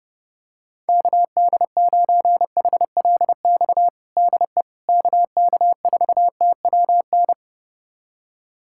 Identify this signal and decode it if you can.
Morse code, heard as a keyed tone, about 30 wpm, 700 Hz. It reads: KD9HLX DE KK4TWN
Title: USA Callsign DE Callsign - 6 Characters - 30wpm